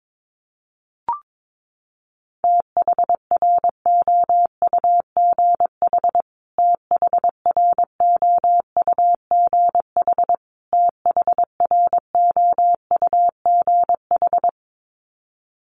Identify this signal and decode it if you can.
Morse code, heard as a keyed tone, about 22 wpm, 700 Hz. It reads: THROUGH THROUGH THROUGH
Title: Top 300 Words Repeated 22wpm